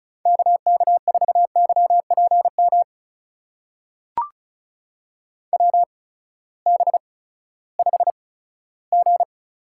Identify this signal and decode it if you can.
Morse code, heard as a keyed tone, about 36 wpm, spaced as for 18 wpm, 700 Hz. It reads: KK4YPM WB5G
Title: USA Call Signs - 6 Characters - 8x Character Spacing - 35wpm